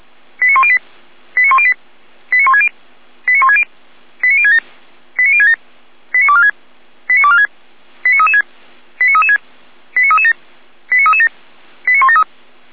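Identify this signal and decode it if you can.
Five-tone selective call (ZVEI), heard as quick runs of five tones, tones of about 70 ms. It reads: 89108 89108 89160 89160 89067 89067 89246 89246 89336 89336 89337 89337 89173
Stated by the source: ZVEI five-tone selective call on 455 received by SDR.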